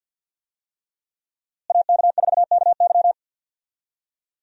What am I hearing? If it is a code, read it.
Morse code, heard as a keyed tone, about 52 wpm, 700 Hz. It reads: AK4KY
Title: USA Call Signs - 5 Characters - Repeated 3x - 50wpm